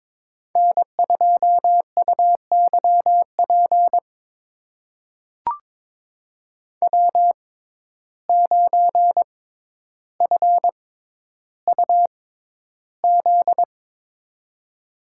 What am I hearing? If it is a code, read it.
Morse code, heard as a keyed tone, about 22 wpm, spaced as for 11 wpm, 700 Hz. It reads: N2UYP W9FUZ
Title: USA Call Signs - 5 Characters - 6x Character Spacing - 22wpm